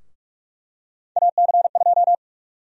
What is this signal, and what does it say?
Morse code, heard as a keyed tone, about 42 wpm, 700 Hz. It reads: AC2
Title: USA Call Sign Prefixes - 45wpm